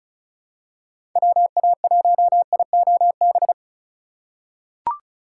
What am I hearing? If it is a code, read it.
Morse code, heard as a keyed tone, about 34 wpm, 700 Hz. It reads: WA1IOB
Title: USA Call Signs - 6 Characters - 3x Character Spacing - 35wpm